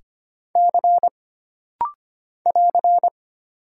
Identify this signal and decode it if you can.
Morse code, heard as a keyed tone, about 25 wpm, 700 Hz. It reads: C <AR>
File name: Single Character (Letter:Number:Punctuation:Prosign) - Rapid-Fire - 25wpm